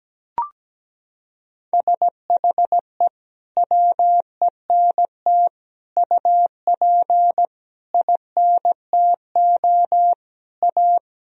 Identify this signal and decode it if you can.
Morse code, heard as a keyed tone, about 17 wpm, 700 Hz. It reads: SHE WENT UP INTO A
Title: Sentences from Top 1000 Words - Review Part 1 - 17wpm